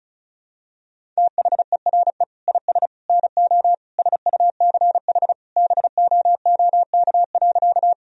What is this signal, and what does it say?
Morse code, heard as a keyed tone, about 36 wpm, 700 Hz. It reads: THERE IS NO SUCH BOOK.
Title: Sentences from Top 1000 Words - Review Part 1 - 35wpm